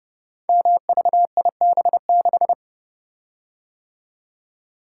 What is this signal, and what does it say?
Morse code, heard as a keyed tone, about 30 wpm, 700 Hz. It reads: MVIB6
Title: Code Groups - 5 Letter:Numbers - 30wpm